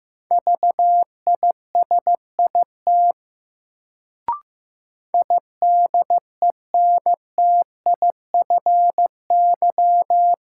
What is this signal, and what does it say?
Morse code, heard as a keyed tone, about 16 wpm, 700 Hz. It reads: VISIT IDENTIFY
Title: Sentences from Top 900 Words - New Word Drill - 15wpm